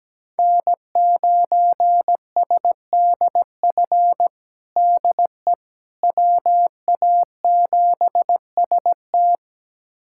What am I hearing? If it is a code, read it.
Morse code, heard as a keyed tone, about 17 wpm, 700 Hz. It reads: N9SDF DE WA7ST
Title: USA Callsign DE Callsign - 5 Characters - 17wpm